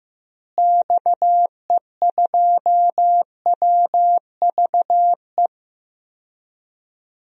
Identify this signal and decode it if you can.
Morse code, heard as a keyed tone, about 15 wpm, 700 Hz. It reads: XE2WVE
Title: DXCC-100 Call Signs - 6 Characters - Repeated 2x - 15wpm